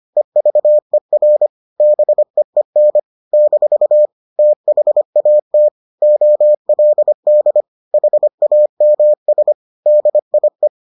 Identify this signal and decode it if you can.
Morse code, heard as a keyed tone, about 25 wpm, 600 Hz. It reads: EVER BEEN - THAT OLD HAMS DIE